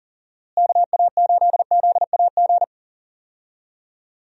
Morse code, heard as a keyed tone, about 40 wpm, 700 Hz. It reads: KA8ZAG
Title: USA Call Signs - 6 Characters - Repeated 2x - 40wpm